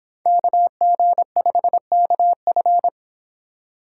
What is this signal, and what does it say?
Morse code, heard as a keyed tone, about 26 wpm, 700 Hz. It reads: KG5KF